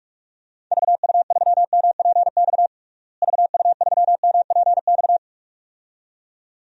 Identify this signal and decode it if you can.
Morse code, heard as a keyed tone, about 46 wpm, 700 Hz. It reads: VU3MPX VU3MPX
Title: DXCC-100 Call Signs - 6 Characters - Repeated 2x - 45wpm